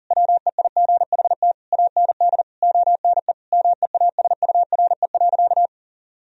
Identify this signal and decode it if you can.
Morse code, heard as a keyed tone, about 40 wpm, 700 Hz. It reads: WEIGHT AND ONE MEASURE.
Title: Sentences from Top 1000 Words - Review Part 3 - 40wpm